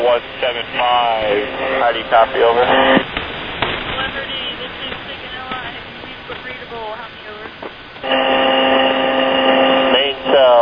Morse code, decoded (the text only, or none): none